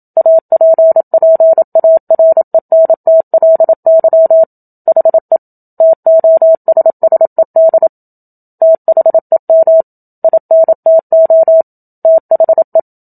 Morse code, heard as a keyed tone, about 27 wpm, 650 Hz. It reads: APPARENTLY HE TOSSED THEM INTO THE
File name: Book Two / 01-Chapter 1 - 27 WPM